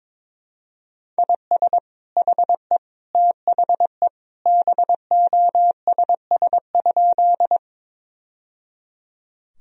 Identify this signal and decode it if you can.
Morse code, heard as a keyed tone, about 22 wpm, 700 Hz. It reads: IS HE THE BOSS?
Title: Sentences from Top 1000 Words - Review Part 4 - 22wpm